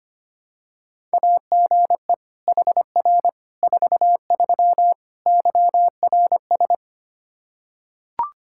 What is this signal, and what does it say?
Morse code, heard as a keyed tone, about 25 wpm, 700 Hz. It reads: AGE HR 43 YRS
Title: Practice QSO - Exchange 3 - More Info - 25wpm